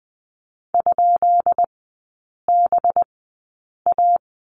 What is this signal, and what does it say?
Morse code, heard as a keyed tone, about 20 wpm, spaced as for 8 wpm, 700 Hz. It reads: ?BA
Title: 184 Lesson 30e - Letter b - Three Characters 20wpm